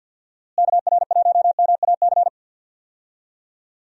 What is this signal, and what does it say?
Morse code, heard as a keyed tone, about 52 wpm, 700 Hz. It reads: KR1MAC
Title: USA Call Signs - 6 Characters - Repeated 3x - 50wpm